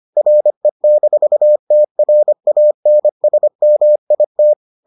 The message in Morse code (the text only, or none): RE-TRANSMIT